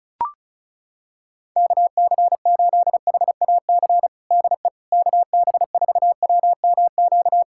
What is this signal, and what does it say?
Morse code, heard as a keyed tone, about 36 wpm, 700 Hz. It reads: KC8HAC DE KB4WMQ
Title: USA Callsign DE Callsign - 6 Characters - 35wpm